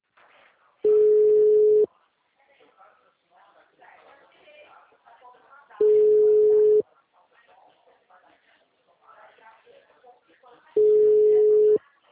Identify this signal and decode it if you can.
Ringback tone (Europe, 425 Hz), with speech.